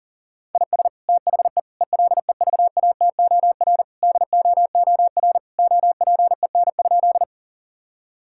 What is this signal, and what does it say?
Morse code, heard as a keyed tone, about 40 wpm, 700 Hz. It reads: IS THE ELEVATOR DOOR OPEN?
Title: Sentences from Top 1600 Words 40wpm